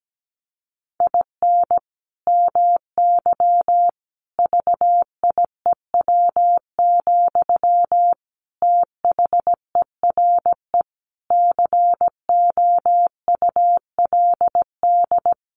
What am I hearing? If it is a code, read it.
Morse code, heard as a keyed tone, about 17 wpm, 700 Hz. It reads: IN MY VIEW, THERE COULD
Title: Sentences from Top 1500 Words 17wpm